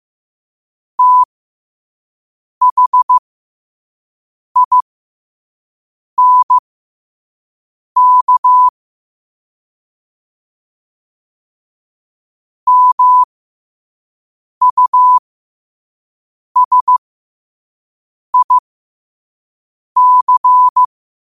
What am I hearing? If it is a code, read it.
Morse code, heard as a keyed tone, about 15 wpm, spaced as for 5 wpm, 1000 Hz. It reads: THINK MUSIC
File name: qotdfiles / 5wpm